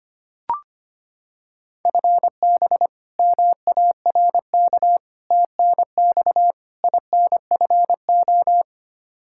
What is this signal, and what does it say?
Morse code, heard as a keyed tone, about 25 wpm, 700 Hz. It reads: FB MARK TNX INFO
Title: K1USN SST QSO Practice 25wpm